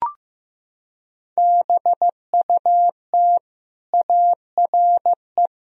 Morse code, heard as a keyed tone, about 16 wpm, 700 Hz. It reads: BUT ARE
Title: Sentences from Top 1000 Words - Review Part 1 - 15wpm